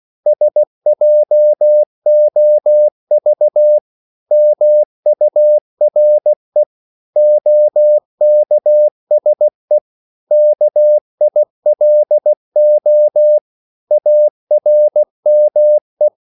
Morse code, heard as a keyed tone, about 16 wpm, 600 Hz. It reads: SJOV MURE OKSE KILO ARME
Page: Korte danske ord 16wpm